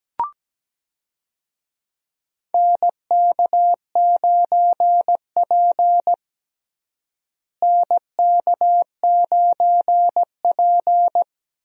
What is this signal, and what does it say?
Morse code, heard as a keyed tone, about 17 wpm, 700 Hz. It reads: NK9P NK9P
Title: USA Call Signs - 4 Characters - Repeated 2x - 17wpm